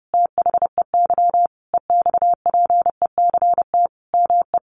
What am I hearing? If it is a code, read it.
Morse code, heard as a keyed tone, about 30 wpm, 700 Hz. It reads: THEY EXPECT ME
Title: Sentences from Top 1100 Words 30wpm